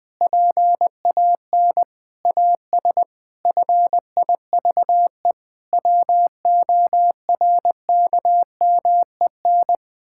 Morse code, heard as a keyed tone, about 20 wpm, 700 Hz. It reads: PAN AS FIVE WORKMEN
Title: Pangram Challenge v2 - 20wpm